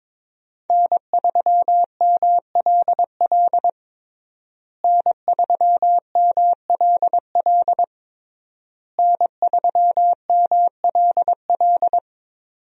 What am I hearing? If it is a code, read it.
Morse code, heard as a keyed tone, about 22 wpm, 700 Hz. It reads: N3MLL N3MLL N3MLL
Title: USA Call Signs - 5 Characters - Repeated 3x - 22wpm